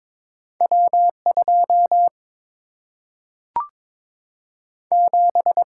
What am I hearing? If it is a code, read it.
Morse code, heard as a keyed tone, about 22 wpm, 700 Hz. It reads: W 2 7
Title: Single-Letter Single-Number 22wpm